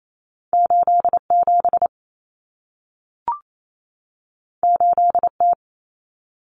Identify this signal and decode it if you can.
Morse code, heard as a keyed tone, about 28 wpm, 700 Hz. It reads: 87 8T
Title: Common Cut Numbers - 2-Digit 28wpm